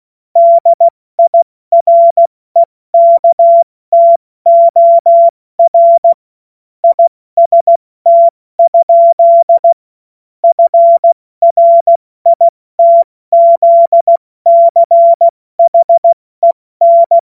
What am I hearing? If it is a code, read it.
Morse code, heard as a keyed tone, about 16 wpm, spaced as for 14 wpm, 680 Hz. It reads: DIREKTOR IST? FRITZCHEN
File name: Morsekurs / Witze / Witz 10. August 2021, 70 F80